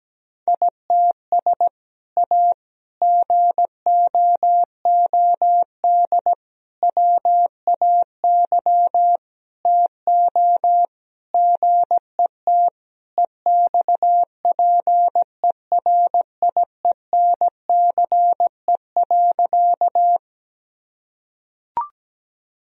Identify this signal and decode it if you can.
Morse code, heard as a keyed tone, about 17 wpm, 700 Hz. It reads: ITS A GOOD WAY TO GET EXPERIENCE.
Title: Sentences from Top 1000 Words - Review Part 3 - 17wpm